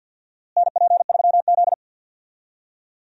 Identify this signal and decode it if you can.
Morse code, heard as a keyed tone, about 52 wpm, 700 Hz. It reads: NP3Z
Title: USA Call Signs - 4 Characters - Repeated 3x - 50wpm